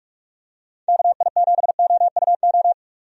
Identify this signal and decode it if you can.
Morse code, heard as a keyed tone, about 46 wpm, 700 Hz. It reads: KI7OUO
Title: USA Call Signs - 6 Characters - 45wpm